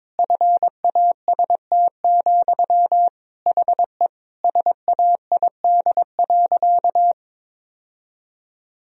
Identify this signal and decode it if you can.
Morse code, heard as a keyed tone, about 23 wpm, 700 Hz. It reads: FAST, HE SAID.
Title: Sentences from Top 1000 Words - Review Part 2 - 22wpm